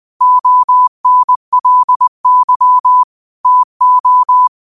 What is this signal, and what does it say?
Morse code, heard as a keyed tone, about 20 wpm, 1000 Hz. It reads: ONLY TO